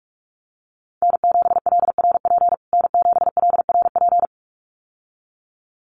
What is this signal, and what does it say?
Morse code, heard as a keyed tone, about 44 wpm, 700 Hz. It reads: N7LRP N7LRP
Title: USA Call Signs Repeated 2x 45wpm